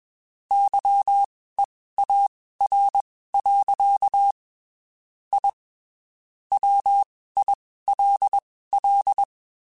Morse code, heard as a keyed tone, about 21 wpm, spaced as for 15 wpm, 800 Hz. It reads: YEAR. I WILL
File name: (spaced) / QSOs Spaced-mp3s / WD0FBT @20 sp